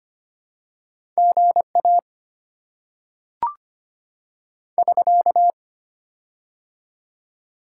Morse code, heard as a keyed tone, about 25 wpm, 690 Hz. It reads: G A <SK>
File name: Top 90 QSO Elements - 6x Character Spacing - 25wpm